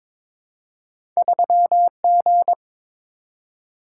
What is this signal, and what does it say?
Morse code, heard as a keyed tone, about 22 wpm, 700 Hz. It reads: 3G
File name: Two Letter-Number Combinations - 22wpm